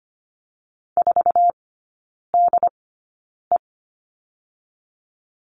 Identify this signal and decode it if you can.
Morse code, heard as a keyed tone, about 25 wpm, spaced as for 9 wpm, 700 Hz. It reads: 4DE
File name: Lesson 17e - Letter u - Three Characters 25wpm